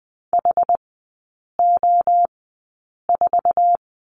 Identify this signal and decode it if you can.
Morse code, heard as a keyed tone, about 20 wpm, spaced as for 8 wpm, 700 Hz. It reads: HO4
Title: 060 Lesson 11e - Letter h - Three Characters 20wpm